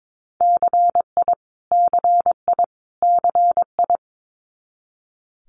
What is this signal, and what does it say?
Morse code, heard as a keyed tone, about 22 wpm, 700 Hz. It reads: CI CI CI
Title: Multi-letter Phonemes Repeated 3x 22wpm